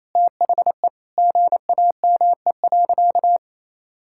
Morse code, heard as a keyed tone, about 28 wpm, 700 Hz. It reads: THE GAME.
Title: Sentences from Top 1000 Words - Review Part 6 - 28wpm